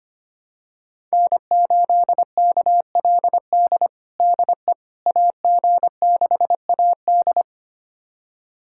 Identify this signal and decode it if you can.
Morse code, heard as a keyed tone, about 25 wpm, 700 Hz. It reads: N8KLD DE AG6AD
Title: USA Callsign DE Callsign - 5 Characters - 25wpm